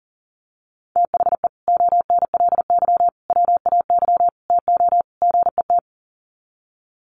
Morse code, heard as a keyed tone, about 40 wpm, 700 Hz. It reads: THE ONLY WAY TO GET
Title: Sets of 5 Words Encore2 40wpm